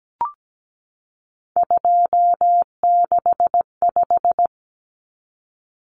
Morse code, heard as a keyed tone, about 17 wpm, 700 Hz. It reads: 265
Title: Numbers - 3-Digit 17wpm